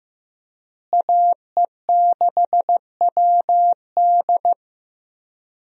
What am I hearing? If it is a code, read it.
Morse code, heard as a keyed tone, about 16 wpm, 700 Hz. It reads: AE6WD
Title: USA Call Signs - 5 Characters - Repeated 3x - 15wpm